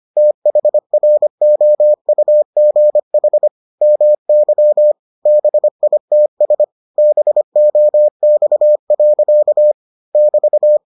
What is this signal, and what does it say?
Morse code, heard as a keyed tone, about 25 wpm, 600 Hz. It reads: THROUGH MY BITS BOX. <BT>